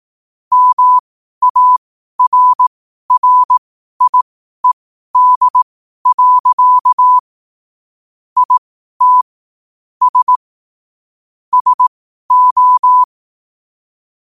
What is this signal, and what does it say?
Morse code, heard as a keyed tone, about 18 wpm, spaced as for 12 wpm, 1000 Hz. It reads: MARRIED. IT S SO
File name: qotdfiles / 12wpm